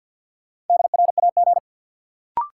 Morse code, heard as a keyed tone, about 53 wpm, 700 Hz. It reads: DRAG
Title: Sentences from Top 1500 Words - New Word Drill - 50wpm